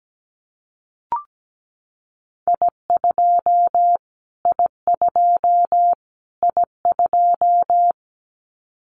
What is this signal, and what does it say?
Morse code, heard as a keyed tone, about 17 wpm, 700 Hz. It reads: I2 I2 I2
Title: Callsign Prefixes for Top 25 DXCC - Repeated 3x - 17wpm